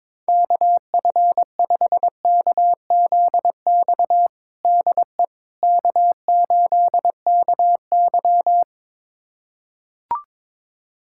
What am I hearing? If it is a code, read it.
Morse code, heard as a keyed tone, about 22 wpm, 700 Hz. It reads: KF5KZX DE K8KY
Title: USA Callsign DE Callsign 22wpm